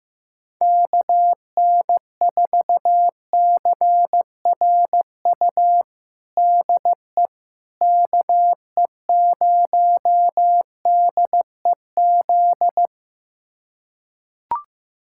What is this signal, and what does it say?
Morse code, heard as a keyed tone, about 15 wpm, 700 Hz. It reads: KN4CRU DE KE0DEZ
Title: USA Callsign DE Callsign - 6 Characters - 15wpm